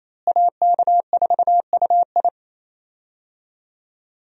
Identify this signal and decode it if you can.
Morse code, heard as a keyed tone, about 27 wpm, 700 Hz. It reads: AK4UI
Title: USA Call Signs - 5 Characters - Repeated 3x - 28wpm